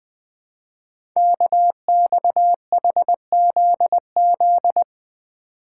Morse code, heard as a keyed tone, about 20 wpm, 700 Hz. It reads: KXHZZ
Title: Code Groups - 5 Letters - 20wpm